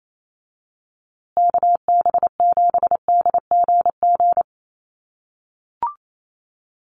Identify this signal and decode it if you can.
Morse code, heard as a keyed tone, about 28 wpm, 700 Hz. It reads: KB7DGG
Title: USA Call Signs Repeated 2x 28wpm